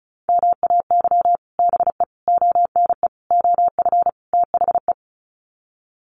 Morse code, heard as a keyed tone, about 34 wpm, 700 Hz. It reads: MAY BE ONE OF THE
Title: Sets of 5 Words Encore2 35wpm